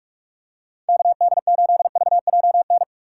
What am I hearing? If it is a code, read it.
Morse code, heard as a keyed tone, about 45 wpm, 700 Hz. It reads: KD8VJN
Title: USA Call Signs - 6 Characters - 2x Character Spacing - 45wpm